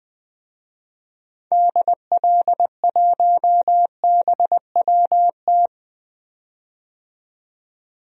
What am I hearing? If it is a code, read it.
Morse code, heard as a keyed tone, about 20 wpm, 700 Hz. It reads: DL1BWT
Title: DXCC-100 Call Signs - 6 Characters - 20wpm